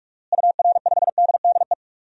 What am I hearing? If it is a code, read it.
Morse code, heard as a keyed tone, about 46 wpm, 700 Hz. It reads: UR5DDE
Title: DXCC-100 Call Signs - 6 Characters - Repeated 3x - 45wpm